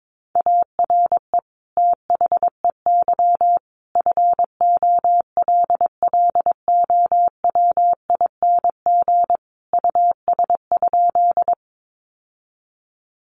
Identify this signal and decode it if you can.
Morse code, heard as a keyed tone, about 22 wpm, 700 Hz. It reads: ARE THEY FOLLOWING US?
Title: Sentences from Top 700 Words 22wpm